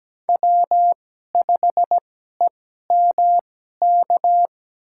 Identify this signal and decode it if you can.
Morse code, heard as a keyed tone, about 17 wpm, spaced as for 12 wpm, 700 Hz. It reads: W5EMK
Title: USA Call Signs - 5 Characters - 2x Character Spacing - 17wpm